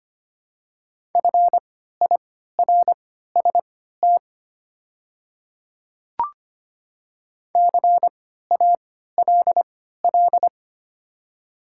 Morse code, heard as a keyed tone, about 25 wpm, spaced as for 14 wpm, 690 Hz. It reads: FIRST CALL
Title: 3x Character Spacing - Top 100 Words - 25wpm